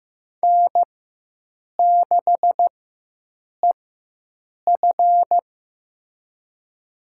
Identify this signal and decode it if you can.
Morse code, heard as a keyed tone, about 15 wpm, spaced as for 7 wpm, 700 Hz. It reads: N6EF
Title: USA Call Signs - 4 Characters - 4x Character Spacing - 15wpm